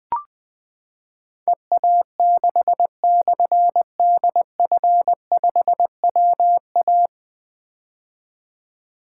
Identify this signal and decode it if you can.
Morse code, heard as a keyed tone, about 20 wpm, 700 Hz. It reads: EA6/DF5WA
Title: Difficult Call Signs 20wpm